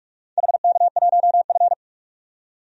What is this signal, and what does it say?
Morse code, heard as a keyed tone, about 44 wpm, 700 Hz. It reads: HK1F
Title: DXCC-100 Call Signs - 4 Characters - 45wpm